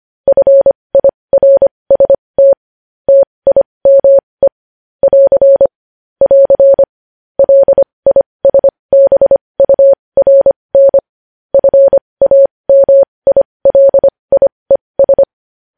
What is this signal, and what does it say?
Morse code, heard as a keyed tone, about 25 wpm, spaced as for 20 wpm, 550 Hz. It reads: FIRST TIME <AR> <AR> LISBURN FAMILIES
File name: podcast / News headlines-1x-25x20